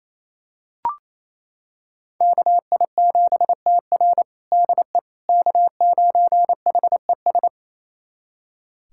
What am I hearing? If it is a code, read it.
Morse code, heard as a keyed tone, about 28 wpm, 700 Hz. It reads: KI7TR DE K9HES
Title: USA Callsign DE Callsign - 5 Characters - 28wpm